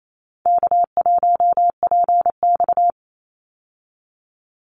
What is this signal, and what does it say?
Morse code, heard as a keyed tone, about 28 wpm, 700 Hz. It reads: K1PX
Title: Easy Call Signs Repeated 3x - 28wpm